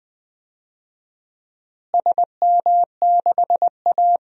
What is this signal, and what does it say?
Morse code, heard as a keyed tone, about 20 wpm, 700 Hz. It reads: SM6A
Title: DXCC-100 Call Signs - 4 Characters - Repeated 2x - 20wpm